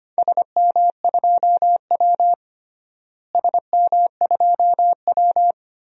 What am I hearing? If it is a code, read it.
Morse code, heard as a keyed tone, about 25 wpm, 700 Hz. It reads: SM2W SM2W
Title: DXCC-100 Call Signs - 4 Characters - Repeated 3x - 25wpm